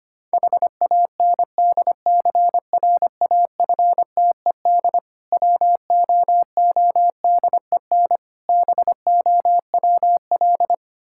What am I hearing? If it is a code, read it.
Morse code, heard as a keyed tone, about 25 wpm, 690 Hz. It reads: HANDCRAFTED WOODEN BOWL
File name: practice / 25wpm / Difficult Words with Hints - 25wpm